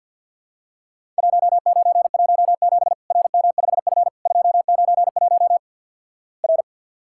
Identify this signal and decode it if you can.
Morse code, heard as a keyed tone, about 50 wpm, 700 Hz. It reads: 1917 RM5F 291 R
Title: Worked All Europe DX Contest - QTC Exchange - 50wpm